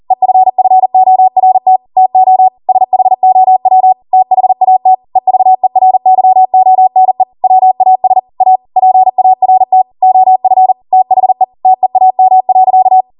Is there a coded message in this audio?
Yes: Morse code from 0.1 to 13.0 s, heard as a keyed tone, about 40 wpm, 750 Hz. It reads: EFFORT TO SHOW THAT EVERYONE WAS A PART OF THE TEAM.